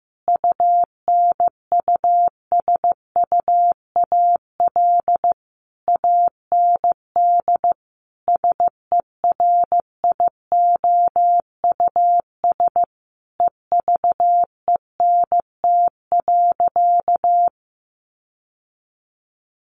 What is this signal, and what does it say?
Morse code, heard as a keyed tone, about 15 wpm, 700 Hz. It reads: UNUSUAL AND SERIOUS EVENT.
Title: Sentences from Top 1400 Words 15wpm